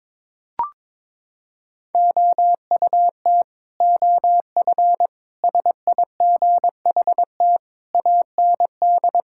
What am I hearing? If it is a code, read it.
Morse code, heard as a keyed tone, about 22 wpm, 700 Hz. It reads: OUT OF SIGHT AND
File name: Sentences with 7-Word Phrases - 22wpm